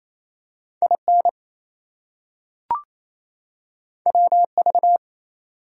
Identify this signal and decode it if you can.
Morse code, heard as a keyed tone, about 28 wpm, 700 Hz. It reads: IN WV
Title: US States and Canadian Provinces 28wpm